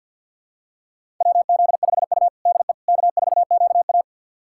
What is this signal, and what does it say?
Morse code, heard as a keyed tone, about 50 wpm, 700 Hz. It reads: WZ5U DE K4QA
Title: USA Callsign DE Callsign - 4 Characters - 50wpm